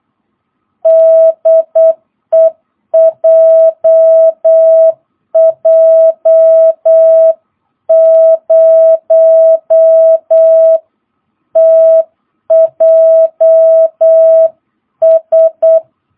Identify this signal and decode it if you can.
Morse code, heard as a keyed tone, about 8 wpm, 660 Hz. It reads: DE JJ0TJS